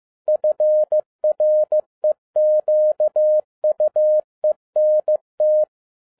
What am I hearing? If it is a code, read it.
Morse code, heard as a keyed tone, about 15 wpm, 610 Hz. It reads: FREQUENT
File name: 2011-07-mp3 / n12 T8XO 15wpm